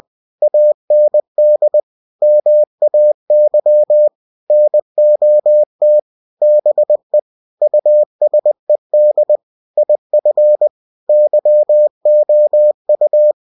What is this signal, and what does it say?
Morse code, heard as a keyed tone, about 20 wpm, 600 Hz. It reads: AND MAY NOT BE USED IF YOU
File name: cw-books / winnie-the-pooh / winnie-20wpm-page-0112